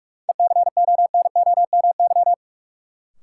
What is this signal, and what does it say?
Morse code, heard as a keyed tone, about 41 wpm, 700 Hz. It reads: ECONOMY
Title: Sentences from Top 1200 Words - New Word Drill Alt - 45wpm